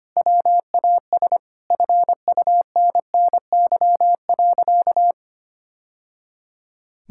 Morse code, heard as a keyed tone, about 25 wpm, 700 Hz. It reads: WAS FUNNY.
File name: practice / 25wpm / Sentences with 2-Word Phrases - 25wpm